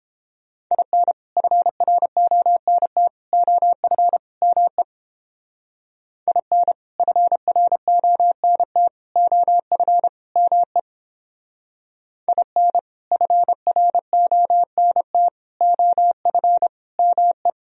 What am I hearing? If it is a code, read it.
Morse code, heard as a keyed tone, about 29 wpm, 700 Hz. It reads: IN FRONT OF ME IN FRONT OF ME IN FRONT OF ME